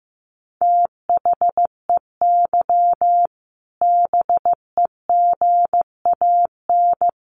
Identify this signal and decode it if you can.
Morse code, heard as a keyed tone, about 15 wpm, 700 Hz. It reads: THEY BEGAN
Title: Sentences from Top 1100 Words 15wpm